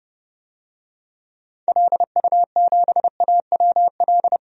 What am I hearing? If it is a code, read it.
Morse code, heard as a keyed tone, about 32 wpm, 700 Hz. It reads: LU7AWL
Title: DXCC-100 Call Signs - 6 Characters - 30wpm